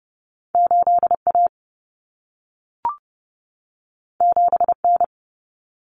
Morse code, heard as a keyed tone, about 30 wpm, 700 Hz. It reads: 8A 7N
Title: Common Cut Numbers - 2-Digit 30wpm